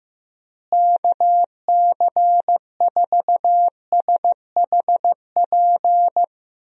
Morse code, heard as a keyed tone, about 15 wpm, 700 Hz. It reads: KC4SHP
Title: USA Call Signs - 6 Characters - Repeated 3x - 15wpm